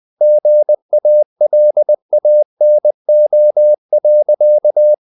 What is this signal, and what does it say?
Morse code, heard as a keyed tone, about 20 wpm, 600 Hz. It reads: GALANO.
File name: the old man and the sea part 2 20wpm